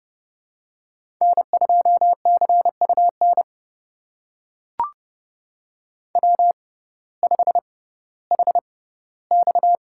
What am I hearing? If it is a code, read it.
Morse code, heard as a keyed tone, about 30 wpm, spaced as for 17 wpm, 700 Hz. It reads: N2CUN W5HX
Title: USA Call Signs - 6x Character Spacing - 30wpm